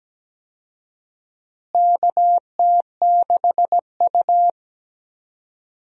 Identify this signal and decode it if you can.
Morse code, heard as a keyed tone, about 17 wpm, 700 Hz. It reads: KT6U
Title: USA Call Signs - 4 Characters - Repeated 2x - 17wpm